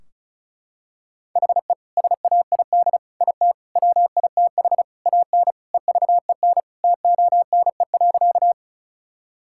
Morse code, heard as a keyed tone, about 34 wpm, 700 Hz. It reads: HE SAID IT WITH AN EVEN TONE.
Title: Sentences from Top 1600 Words 35wpm